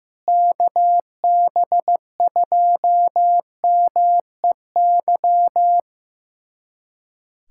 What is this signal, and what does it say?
Morse code, heard as a keyed tone, about 16 wpm, 700 Hz. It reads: KB2MEY
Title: USA Callsign DE Callsign - 6 Characters - 15wpm